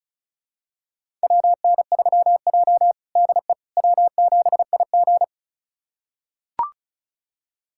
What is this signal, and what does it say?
Morse code, heard as a keyed tone, about 36 wpm, 700 Hz. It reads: WN3J DE W7IG
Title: USA Callsign DE Callsign - 4 Characters - 35wpm